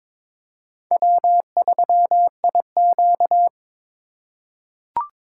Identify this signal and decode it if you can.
Morse code, heard as a keyed tone, about 22 wpm, 700 Hz. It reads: W3IQ
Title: USA Call Signs - 4 Characters - 2x Character Spacing - 22wpm